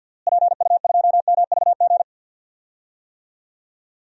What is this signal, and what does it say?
Morse code, heard as a keyed tone, about 52 wpm, 700 Hz. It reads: PU2MVG
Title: DXCC-100 Call Signs - 6 Characters - Repeated 2x - 50wpm